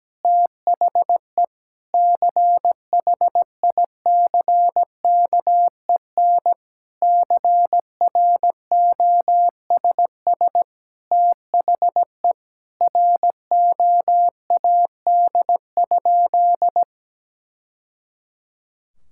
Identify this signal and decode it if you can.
Morse code, heard as a keyed tone, about 17 wpm, 700 Hz. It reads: THE CHICKEN CROSS THE ROAD?
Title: Sentences from Top 1000 Words - Review Part 5 - 17wpm